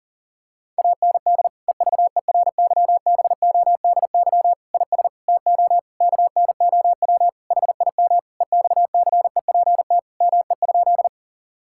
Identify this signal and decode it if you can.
Morse code, heard as a keyed tone, about 40 wpm, 700 Hz. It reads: AND EVERYBODY IS TO KNOW HIM EXCEPT ME?
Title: Sentences from Top 1600 Words 40wpm